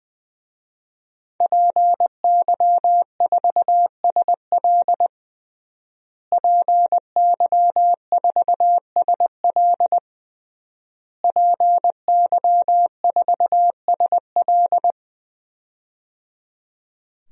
Morse code, heard as a keyed tone, about 21 wpm, 700 Hz. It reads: PY4SL PY4SL PY4SL
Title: DXCC-100 Call Signs - 5 Characters - Repeated 3x - 20wpm